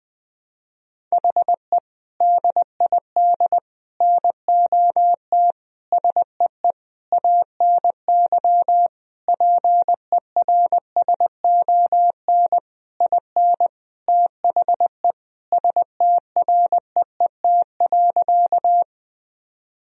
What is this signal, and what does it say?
Morse code, heard as a keyed tone, about 20 wpm, 700 Hz. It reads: HE DID NOT SEE ANY PERSON IN THE STREET.
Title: Sentences from Top 1000 Words - Review Part 3 - 20wpm